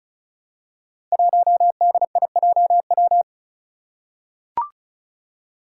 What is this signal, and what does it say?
Morse code, heard as a keyed tone, about 36 wpm, 700 Hz. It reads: 1DIJW
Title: Five Challenging Letter-Number Combinations 35wpm